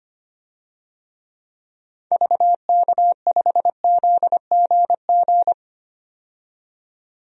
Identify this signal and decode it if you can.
Morse code, heard as a keyed tone, about 25 wpm, 700 Hz. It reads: VK5ZGG
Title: DXCC-100 Call Signs - 6 Characters - 25wpm